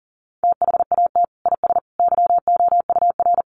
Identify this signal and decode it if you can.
Morse code, heard as a keyed tone, about 40 wpm, 700 Hz. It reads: THAT IS YOUR
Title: Sentences from Top 1200 Words 40wpm